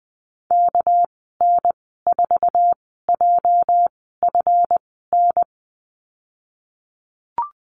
Morse code, heard as a keyed tone, about 19 wpm, spaced as for 14 wpm, 700 Hz. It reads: KN4JFN
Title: 231 Lesson 37g - Letter x - Call Signs 20wpm